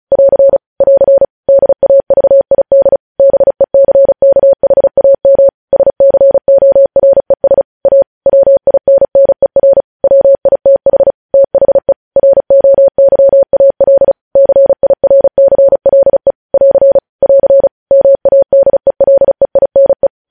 Morse code, heard as a keyed tone, about 36 wpm, 550 Hz. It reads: <AR> <AR> DAVID BECKHAM SCORES A WINNER WITH THE ROYAL CIRCLE <AR> <AR> MADELEINE
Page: News Headlines in Morse Code at 35 WPM